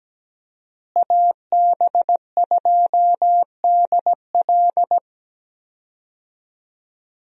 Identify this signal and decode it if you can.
Morse code, heard as a keyed tone, about 17 wpm, 700 Hz. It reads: AB2DL
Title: USA Call Signs - 5 Characters - Repeated 2x - 17wpm